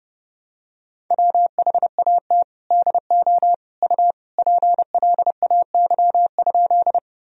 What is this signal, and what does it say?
Morse code, heard as a keyed tone, about 32 wpm, 700 Hz. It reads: WHAT DO U PLAY?
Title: Sentences from Top 200 Words and Abbreviations 30wpm